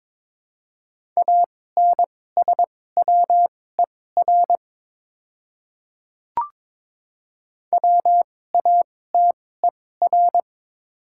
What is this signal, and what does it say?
Morse code, heard as a keyed tone, about 22 wpm, spaced as for 16 wpm, 700 Hz. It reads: ANSWER WATER
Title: 2x Character Spacing - Top 300 Words - 22wpm